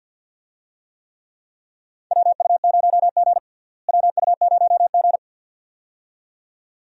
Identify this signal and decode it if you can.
Morse code, heard as a keyed tone, about 51 wpm, 690 Hz. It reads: WU0G WU0G
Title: USA Call Signs - 4 Characters - Repeated 2x - 50wpm